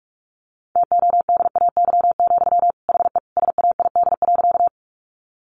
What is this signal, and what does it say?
Morse code, heard as a keyed tone, about 44 wpm, 700 Hz. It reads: TODAY, HE SAID.
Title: Sentences from Top 600 Words 45wpm